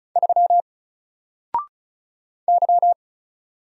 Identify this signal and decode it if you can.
Morse code, heard as a keyed tone, about 36 wpm, 700 Hz. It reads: 3 Y
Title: Single Letter-Number Alt 35wpm